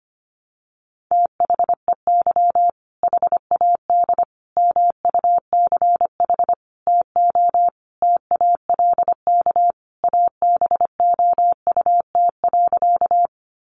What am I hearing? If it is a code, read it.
Morse code, heard as a keyed tone, about 25 wpm, 700 Hz. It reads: THEY HAD MUCH TO TALK ABOUT.
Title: Sentences from Top 400 Words 25wpm